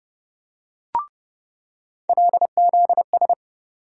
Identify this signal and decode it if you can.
Morse code, heard as a keyed tone, about 30 wpm, 700 Hz. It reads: LZS
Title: USA Call Sign Postfixes - 3 Letters 30wpm